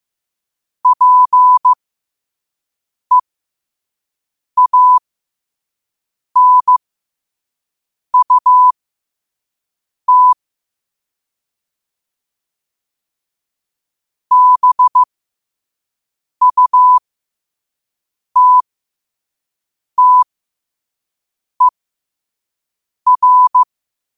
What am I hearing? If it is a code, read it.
Morse code, heard as a keyed tone, about 15 wpm, spaced as for 5 wpm, 1000 Hz. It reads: PEANUT BUTTER
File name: 5wpm